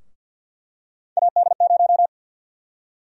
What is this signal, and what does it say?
Morse code, heard as a keyed tone, about 51 wpm, 700 Hz. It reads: AD0
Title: USA Call Sign Prefixes - 50wpm